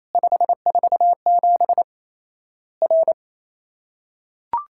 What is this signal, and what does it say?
Morse code, heard as a keyed tone, about 28 wpm, 700 Hz. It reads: 547 R
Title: Worked All Europe DX Contest - QTC Exchange - 28wpm